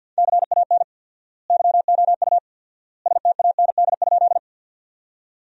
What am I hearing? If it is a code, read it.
Morse code, heard as a keyed tone, about 46 wpm, 700 Hz. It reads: CAN YOU STAND?
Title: 4x Word-Spacing - Sentences from Top 300 Words - 50wpm